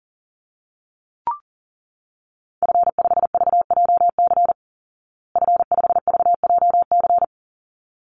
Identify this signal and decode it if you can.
Morse code, heard as a keyed tone, about 40 wpm, 700 Hz. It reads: F5VJC F5VJC
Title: Call Signs Repeated 3x - 40wpm